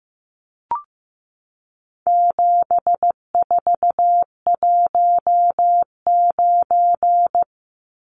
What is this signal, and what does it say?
Morse code, heard as a keyed tone, about 15 wpm, 700 Hz. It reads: 7419
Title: Numbers - 4-Digit 15wpm